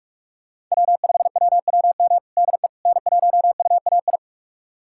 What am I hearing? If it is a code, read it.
Morse code, heard as a keyed tone, about 44 wpm, 700 Hz. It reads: W5WWM DE N1UAI
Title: USA Callsign DE Callsign - 5 Characters - 45wpm